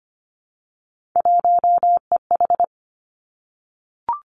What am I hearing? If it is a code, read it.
Morse code, heard as a keyed tone, about 25 wpm, 700 Hz. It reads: 1EH
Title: Three Challenging Letter-Number Combinations 25wpm